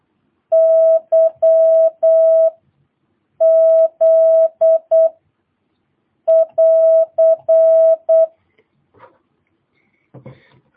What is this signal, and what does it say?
Morse code, heard as a keyed tone, about 8 wpm, 650 Hz. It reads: Y Z <AR>
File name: jh8cbh / 2022cw